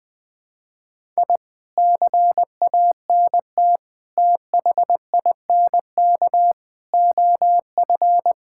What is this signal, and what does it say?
Morse code, heard as a keyed tone, about 20 wpm, 700 Hz. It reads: I CANT THINK OF
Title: Sentences with 8-Word Phrases - Spoken Beginning - 20wpm